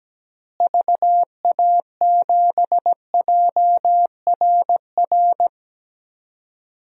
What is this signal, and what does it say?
Morse code, heard as a keyed tone, about 17 wpm, 700 Hz. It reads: VA7JRR
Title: DXCC-100 Call Signs - 6 Characters - Repeated 3x - 17wpm